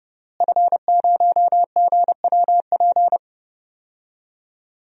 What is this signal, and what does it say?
Morse code, heard as a keyed tone, about 32 wpm, 700 Hz. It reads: F0GWP
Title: DXCC-100 Call Signs - 5 Characters - Repeated 2x - 30wpm